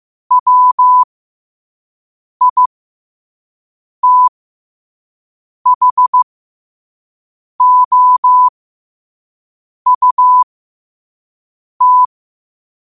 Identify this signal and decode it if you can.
Morse code, heard as a keyed tone, about 15 wpm, spaced as for 5 wpm, 1000 Hz. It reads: WITHOUT
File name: qotdfiles / 5wpm